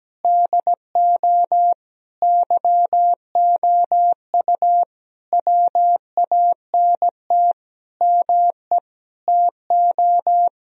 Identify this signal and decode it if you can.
Morse code, heard as a keyed tone, about 17 wpm, 700 Hz. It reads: DO YOU WANT ME TO
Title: Sentences from Top 1000 Words - Review Part 3 - 17wpm